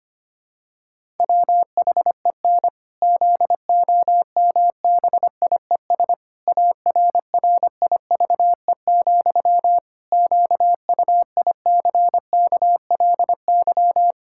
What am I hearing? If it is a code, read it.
Morse code, heard as a keyed tone, about 25 wpm, 700 Hz. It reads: WHEN ZOMBIES ARRIVE, QUICKLY
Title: Pangram Challenge v2 - 25wpm